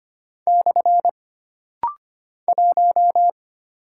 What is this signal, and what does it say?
Morse code, heard as a keyed tone, about 25 wpm, 700 Hz. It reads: / 1
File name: Single Character (Letter:Number:Punctuation:Prosign) - Rapid-Fire - 25wpm